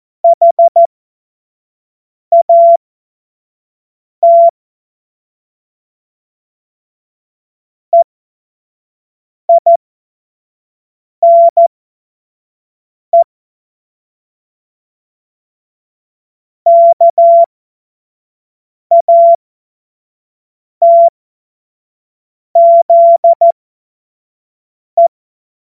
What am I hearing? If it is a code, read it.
Morse code, heard as a keyed tone, about 14 wpm, spaced as for 5 wpm, 680 Hz. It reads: HAT EINE KATZE
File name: Morsekurs / Witze / Witz 05. April 2022 25 F70